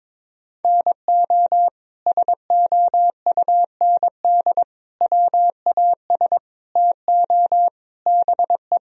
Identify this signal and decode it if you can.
Morse code, heard as a keyed tone, about 22 wpm, 700 Hz. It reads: NO SOUND WAS TO BE
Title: Sentences from Top 1000 Words - Review Part 1 - 22wpm